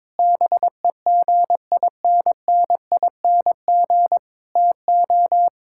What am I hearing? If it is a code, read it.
Morse code, heard as a keyed tone, about 22 wpm, 700 Hz. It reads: BEGINNING TO
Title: Sentences from Top 1000 Words - Review Part 5 - 22wpm